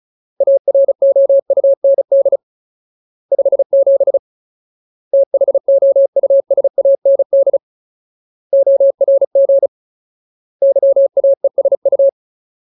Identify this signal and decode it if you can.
Morse code, heard as a keyed tone, about 34 wpm, 550 Hz. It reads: AROUND 57 THOUSAND ORG YAESU
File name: Ultimate Word List / 35WPM Ultimate